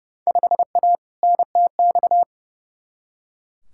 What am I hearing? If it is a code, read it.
Morse code, heard as a keyed tone, about 30 wpm, 700 Hz. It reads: 5A NTX
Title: Field Day Exchanges - 30wpm